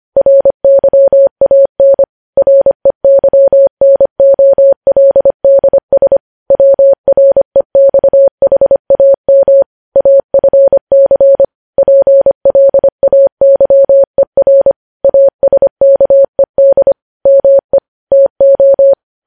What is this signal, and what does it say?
Morse code, heard as a keyed tone, about 25 wpm, 550 Hz. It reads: RYAN REYNOLDS WREXHAM AFC PLAYER ASKED ME TO
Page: news headlines in morse code at 25 wpm